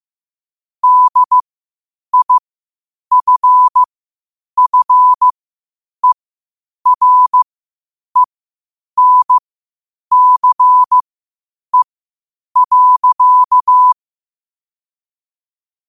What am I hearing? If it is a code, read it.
Morse code, heard as a keyed tone, about 15 wpm, spaced as for 8 wpm, 1000 Hz. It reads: DIFFERENCE.
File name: qotdfiles / 8wpm